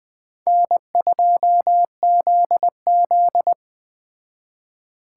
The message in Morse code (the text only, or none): N2ZZ